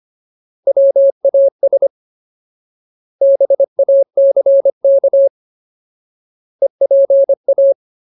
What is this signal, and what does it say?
Morse code, heard as a keyed tone, about 25 wpm, 550 Hz. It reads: WAS BACK EPA
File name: Ultimate Word List / 25WPM Ultimate